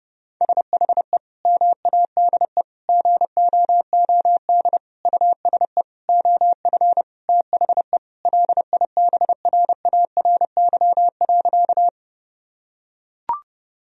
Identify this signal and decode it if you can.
Morse code, heard as a keyed tone, about 30 wpm, 700 Hz. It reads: SHE MADE GOOD USE OF THE LIBRARY.
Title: Sentences from Top 1000 Words - Review Part 4 - 30wpm